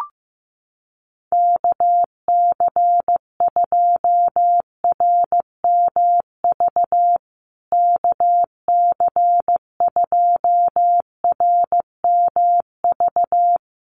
Morse code, heard as a keyed tone, about 14 wpm, 700 Hz. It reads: KC2RMV KC2RMV
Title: USA Call Signs Repeated 3x 15wpm